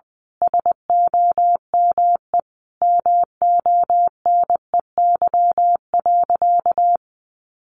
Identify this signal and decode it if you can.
Morse code, heard as a keyed tone, about 19 wpm, 700 Hz. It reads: SOME MONEY.
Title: Sentences from Top 1500 Words 20wpm